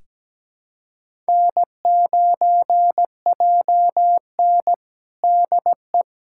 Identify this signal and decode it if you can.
Morse code, heard as a keyed tone, about 17 wpm, 700 Hz. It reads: N9JN DE
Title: USA Callsign DE Callsign - 4 Characters - 17wpm